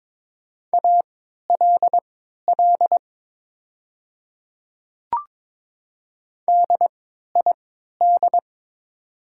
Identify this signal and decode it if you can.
Morse code, heard as a keyed tone, about 22 wpm, spaced as for 13 wpm, 700 Hz. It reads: ALL DID
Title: 3x Character Spacing - Top 100 Words - 22wpm